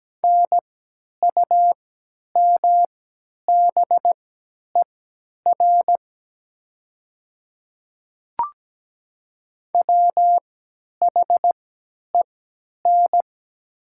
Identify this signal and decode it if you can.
Morse code, heard as a keyed tone, about 17 wpm, spaced as for 10 wpm, 690 Hz. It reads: NUMBER WHEN
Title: 3x Character Spacing - Top 100 Words - 17wpm